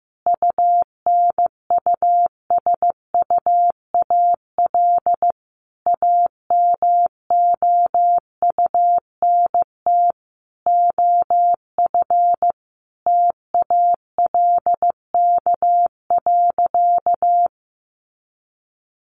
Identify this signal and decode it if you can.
Morse code, heard as a keyed tone, about 15 wpm, 700 Hz. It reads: UNUSUAL AMOUNT OF TALK.
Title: Sentences from Top 1400 Words 15wpm